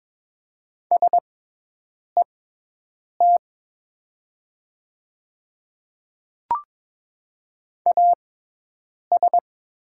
Morse code, heard as a keyed tone, about 22 wpm, spaced as for 8 wpm, 700 Hz. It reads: SET AS
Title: 6x Character Spacing - Top 200 Words - 22wpm